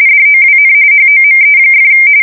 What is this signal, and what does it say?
RTTY, about 45.45 baud. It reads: DOWN HOME!
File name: fsk170 / battleshipnj